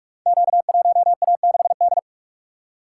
Morse code, heard as a keyed tone, about 44 wpm, 700 Hz. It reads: Q1A6D
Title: Code Groups - 5 Letter:Numbers - 45wpm